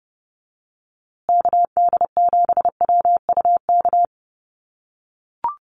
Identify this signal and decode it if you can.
Morse code, heard as a keyed tone, about 30 wpm, 700 Hz. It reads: KD7WUK
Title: USA Call Signs Repeated 3x 30wpm